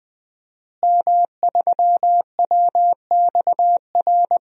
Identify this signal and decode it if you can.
Morse code, heard as a keyed tone, about 20 wpm, 700 Hz. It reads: M3WXR
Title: DXCC-100 Call Signs - 5 Characters - Repeated 3x - 20wpm